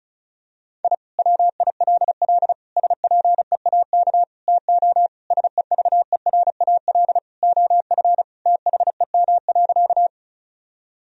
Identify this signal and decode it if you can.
Morse code, heard as a keyed tone, about 36 wpm, 700 Hz. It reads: I WILL SPEAK TO SEVERAL OF THEM.
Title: Sentences from Top 1000 Words - Review Part 2 - 35wpm